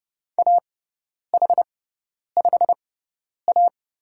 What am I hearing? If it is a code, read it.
Morse code, heard as a keyed tone, about 30 wpm, spaced as for 10 wpm, 700 Hz. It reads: AH5A
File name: Lesson 18g - Letter m - Call Signs 30wpm